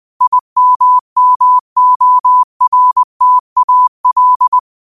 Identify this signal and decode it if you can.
Morse code, heard as a keyed tone, about 20 wpm, 1000 Hz. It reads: IMMORTAL